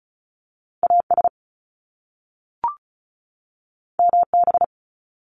Morse code, heard as a keyed tone, about 36 wpm, 700 Hz. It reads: AS MB
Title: Top Two Letter Combinations in Words 35wpm